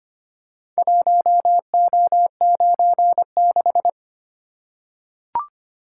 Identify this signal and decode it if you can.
Morse code, heard as a keyed tone, about 25 wpm, 700 Hz. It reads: 1O96
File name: practice / 25wpm / Four Letter-Number Combinations - 25wpm